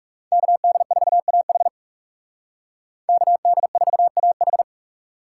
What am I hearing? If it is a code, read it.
Morse code, heard as a keyed tone, about 35 wpm, 700 Hz. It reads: KD4AH KD4AH
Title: Speed Racing - USA 2x2 Call Signs 30wpm